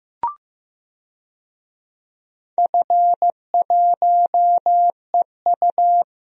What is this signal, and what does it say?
Morse code, heard as a keyed tone, about 15 wpm, 700 Hz. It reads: F1EU
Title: DXCC-100 Call Signs - 4 Characters - 15wpm